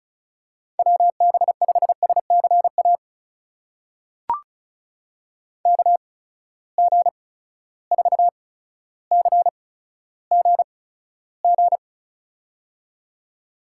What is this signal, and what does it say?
Morse code, heard as a keyed tone, about 36 wpm, spaced as for 15 wpm, 700 Hz. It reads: WB5SCA KG4CGG
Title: USA Call Signs - 6 Characters - 8x Character Spacing - 35wpm